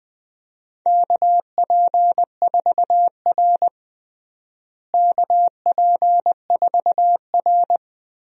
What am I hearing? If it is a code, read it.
Morse code, heard as a keyed tone, about 20 wpm, 700 Hz. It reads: KP4R KP4R
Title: DXCC-100 Call Signs - 4 Characters - Repeated 3x - 20wpm